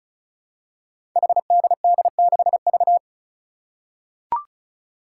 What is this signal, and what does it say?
Morse code, heard as a keyed tone, about 34 wpm, 700 Hz. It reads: HDD6V
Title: Five Challenging Letter-Number Combinations 35wpm